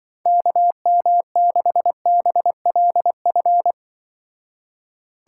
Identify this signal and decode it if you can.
Morse code, heard as a keyed tone, about 23 wpm, 700 Hz. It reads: KM6BLF
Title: Speed Racing - USA 2x3 Call Signs 20wpm